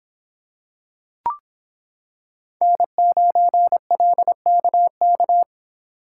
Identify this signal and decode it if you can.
Morse code, heard as a keyed tone, about 25 wpm, 700 Hz. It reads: N9LKK